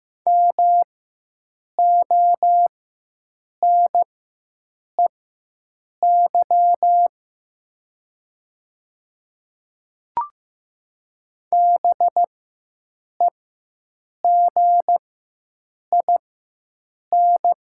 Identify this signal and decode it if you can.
Morse code, heard as a keyed tone, about 15 wpm, spaced as for 7 wpm, 700 Hz. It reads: MONEY BEGIN
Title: 4x Character Spacing - Top 500 Words - 15wpm